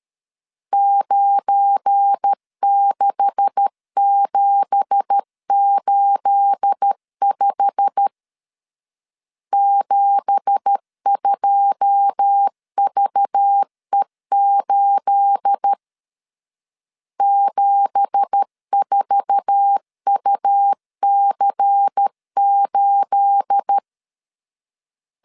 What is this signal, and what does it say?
Morse code, heard as a keyed tone, about 13 wpm, 800 Hz. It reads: 96785 72VE8 74UC8
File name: LEZIONI / 9 2867 12wpm 2322